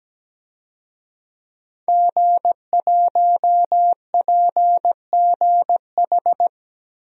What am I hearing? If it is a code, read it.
Morse code, heard as a keyed tone, about 17 wpm, 700 Hz. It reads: G1PGH
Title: DXCC-100 Call Signs - 5 Characters - Repeated 2x - 17wpm